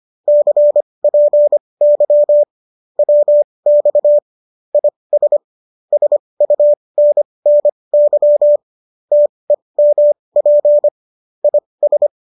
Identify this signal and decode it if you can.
Morse code, heard as a keyed tone, about 25 wpm, spaced as for 20 wpm, 590 Hz. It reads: CPY WX IS SUNNY TEMP IS